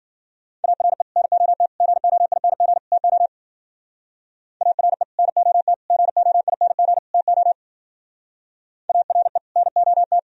Morse code, heard as a keyed tone, about 50 wpm, 700 Hz. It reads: ARE NOT GOING TO ARE NOT GOING TO ARE NOT